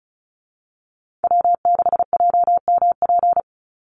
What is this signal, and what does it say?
Morse code, heard as a keyed tone, about 34 wpm, 700 Hz. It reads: W6JMP
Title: Call Signs - 35wpm